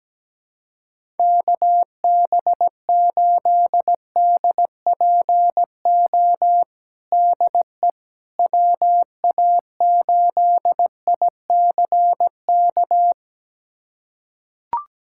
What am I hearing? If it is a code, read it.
Morse code, heard as a keyed tone, about 17 wpm, 700 Hz. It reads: KB8DPO DE WA8ICK
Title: USA Callsign DE Callsign - 6 Characters - 17wpm